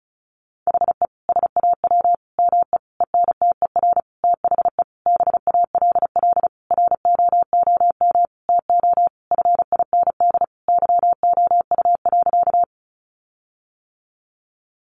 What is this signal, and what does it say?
Morse code, heard as a keyed tone, about 34 wpm, 700 Hz. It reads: HE SAW ME ENTER THE BALL ROOM TO FIND YOU.
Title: Sentences from Top 1300 Words 35wpm